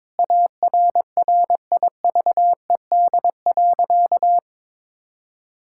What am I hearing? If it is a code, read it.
Morse code, heard as a keyed tone, about 22 wpm, 700 Hz. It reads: ARRIVED.
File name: practice / 22wpm / Sentences with 2-Word Phrases - 22wpm